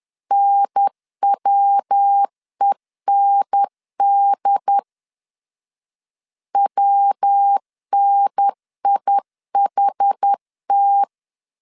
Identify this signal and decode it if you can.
Morse code, heard as a keyed tone, about 10 wpm, 800 Hz. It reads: NWEND WNIHT
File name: CW / DET / 1 3 1500C S B 10wpm 3335